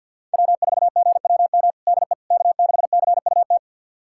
Morse code, heard as a keyed tone, about 48 wpm, 700 Hz. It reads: W4GWM DE K6CUT
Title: USA Callsign DE Callsign - 5 Characters - 50wpm